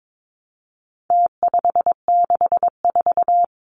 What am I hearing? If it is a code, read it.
Morse code, heard as a keyed tone, about 22 wpm, 700 Hz. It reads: T564
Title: Numbers - Contest Style - 4-Digit 22wpm